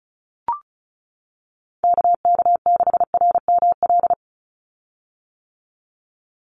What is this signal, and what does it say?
Morse code, heard as a keyed tone, about 34 wpm, 700 Hz. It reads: KK6RML
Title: USA Call Signs 35wpm